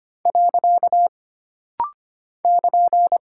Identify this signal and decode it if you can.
Morse code, heard as a keyed tone, about 25 wpm, 700 Hz. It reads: . <KN>
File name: Single Punctuation-Prosign - Rapid-Fire - 25wpm